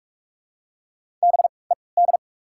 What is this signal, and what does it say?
Morse code, heard as a keyed tone, about 48 wpm, spaced as for 26 wpm, 700 Hz. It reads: BED
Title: 3x Character Spacing - Top 500 Words - 45wpm